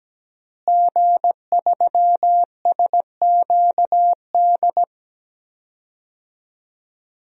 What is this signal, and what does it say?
Morse code, heard as a keyed tone, about 17 wpm, 700 Hz. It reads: G3SQD
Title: DXCC-100 Call Signs - 5 Characters - Repeated 3x - 17wpm